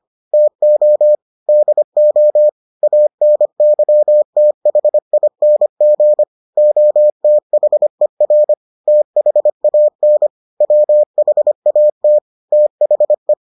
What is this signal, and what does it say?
Morse code, heard as a keyed tone, about 25 wpm, 600 Hz. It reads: TO DO ANYTHING OTHER THAN WHAT THE